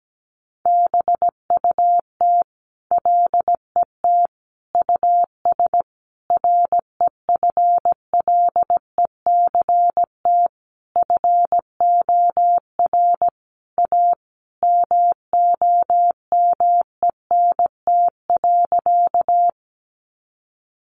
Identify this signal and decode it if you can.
Morse code, heard as a keyed tone, about 17 wpm, 700 Hz. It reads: BUT LET US REFLECT FOR A MOMENT.
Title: Sentences from Top 1500 Words 17wpm